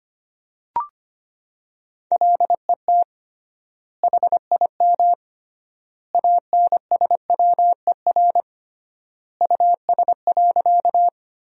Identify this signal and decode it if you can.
Morse code, heard as a keyed tone, about 25 wpm, 700 Hz. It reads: LET HIM ANSWER US.
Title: 3x Word-Spacing - Sentences from Top 300 Words - 25wpm